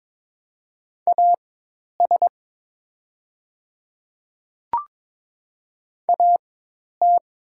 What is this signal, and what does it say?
Morse code, heard as a keyed tone, about 22 wpm, spaced as for 10 wpm, 700 Hz. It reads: AS AT
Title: 4x Character Spacing - Top 100 Words - 22wpm